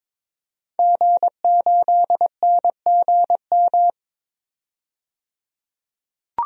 Morse code, heard as a keyed tone, about 21 wpm, 700 Hz. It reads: G8NGM
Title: DXCC-100 Call Signs - 5 Characters - 22wpm